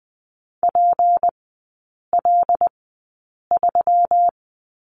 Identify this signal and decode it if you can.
Morse code, heard as a keyed tone, about 20 wpm, spaced as for 8 wpm, 700 Hz. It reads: PL3
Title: 159 Lesson 26e - Letter g - Three Characters 20wpm